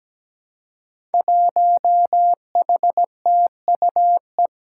Morse code, heard as a keyed tone, about 17 wpm, 700 Hz. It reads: 1HTUE
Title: Five Challenging Letter-Number Combinations 17wpm